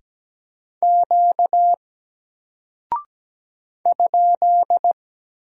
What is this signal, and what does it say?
Morse code, heard as a keyed tone, about 17 wpm, 700 Hz. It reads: Q ?
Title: Single Character (Letter:Number:Punctuation:Prosign) Alt 17wpm